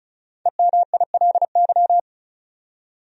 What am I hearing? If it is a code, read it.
Morse code, heard as a keyed tone, about 34 wpm, 700 Hz. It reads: EMILY
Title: USA Common Names - 35wpm